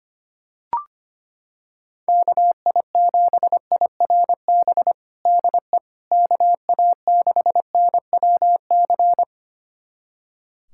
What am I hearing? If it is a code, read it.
Morse code, heard as a keyed tone, about 25 wpm, 700 Hz. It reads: KI7IRB DE KA6NWC
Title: USA Callsign DE Callsign - 6 Characters - 25wpm